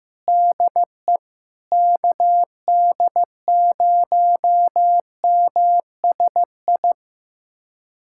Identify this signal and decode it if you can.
Morse code, heard as a keyed tone, about 15 wpm, 700 Hz. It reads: DE KD0MSI
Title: USA Callsign DE Callsign - 6 Characters - 15wpm